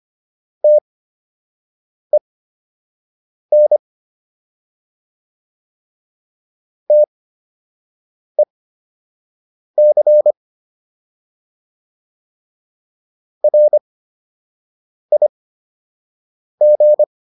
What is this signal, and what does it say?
Morse code, heard as a keyed tone, about 25 wpm, spaced as for 6 wpm, 600 Hz. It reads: TEN TEC RIG